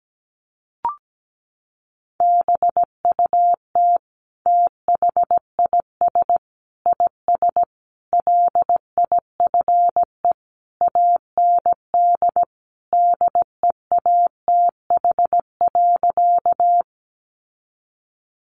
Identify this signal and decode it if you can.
Morse code, heard as a keyed tone, about 17 wpm, 700 Hz. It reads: BUT THIS IS LIFE AND DEATH.
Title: Sentences from Top 1100 Words 17wpm